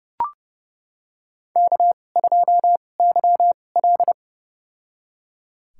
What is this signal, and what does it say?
Morse code, heard as a keyed tone, about 29 wpm, spaced as for 21 wpm, 700 Hz. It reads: K2YL
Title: USA Call Signs - 4 Characters - 2x Character Spacing - 30wpm